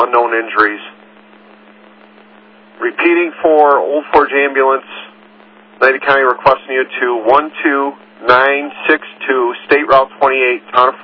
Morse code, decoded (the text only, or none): none